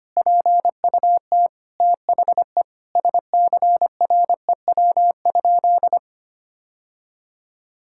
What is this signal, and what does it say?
Morse code, heard as a keyed tone, about 25 wpm, 700 Hz. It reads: PUT THE SCREW?
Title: Sentences from Top 1600 Words 25wpm